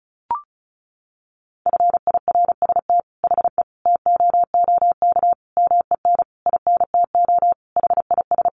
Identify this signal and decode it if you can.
Morse code, heard as a keyed tone, about 34 wpm, 700 Hz. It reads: FIRST HE TOOK MEN INTO HIS
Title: Sentences from Top 1500 Words 35wpm